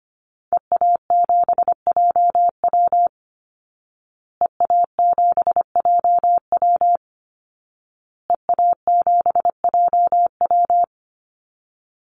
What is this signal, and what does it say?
Morse code, heard as a keyed tone, about 25 wpm, 700 Hz. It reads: EA7JW EA7JW EA7JW
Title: Call Signs Repeated 2x - 25wpm